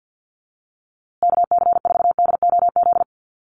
Morse code, heard as a keyed tone, about 50 wpm, 700 Hz. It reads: KC4DOZ
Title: USA Call Signs Repeated 3x 50wpm